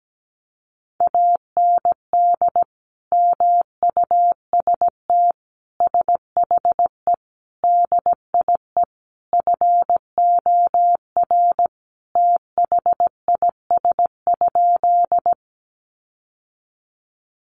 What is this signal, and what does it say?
Morse code, heard as a keyed tone, about 17 wpm, 700 Hz. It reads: AND MUST SHE DIE FOR THIS?
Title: Sentences from Top 1300 Words 17wpm